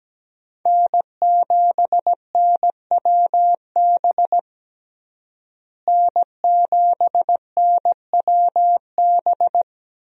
Morse code, heard as a keyed tone, about 17 wpm, 700 Hz. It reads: N7NWB N7NWB
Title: USA Call Signs - 5 Characters - Repeated 2x - 17wpm